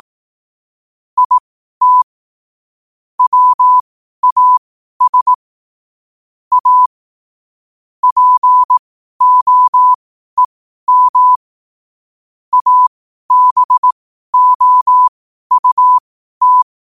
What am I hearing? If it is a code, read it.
Morse code, heard as a keyed tone, about 18 wpm, spaced as for 12 wpm, 1000 Hz. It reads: IT WAS A POEM ABOUT
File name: qotdfiles / 12wpm